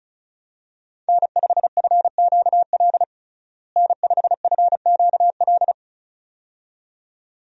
Morse code, heard as a keyed tone, about 34 wpm, 700 Hz. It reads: N5FQL N5FQL
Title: USA Call Signs - 5 Characters - Repeated 2x - 35wpm